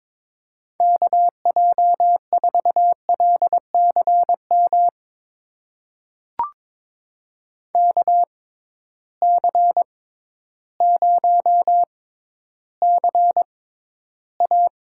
Morse code, heard as a keyed tone, about 22 wpm, spaced as for 12 wpm, 700 Hz. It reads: KJ4LCM KC0CA
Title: USA Call Signs - 6 Characters - 6x Character Spacing - 22wpm